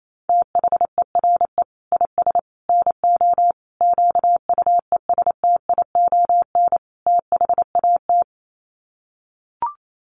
Morse code, heard as a keyed tone, about 28 wpm, 700 Hz. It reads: THERE IS NO QUESTION THAT
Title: Sets of 5 Words Encore 28wpm